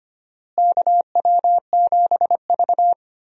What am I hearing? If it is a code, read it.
Morse code, heard as a keyed tone, about 25 wpm, 700 Hz. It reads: KW7V
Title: USA Call Signs - 4 Characters - Repeated 2x - 25wpm